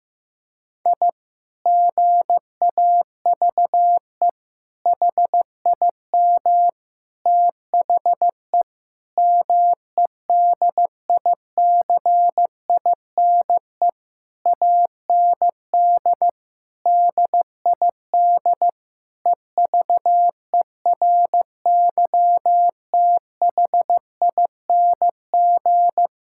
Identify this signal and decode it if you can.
Morse code, heard as a keyed tone, about 15 wpm, 700 Hz. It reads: I GAVE HIM THE MEDICINE AND DID EVERYTHING
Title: Sentences from Top 1000 Words - Review Part 6 - 15wpm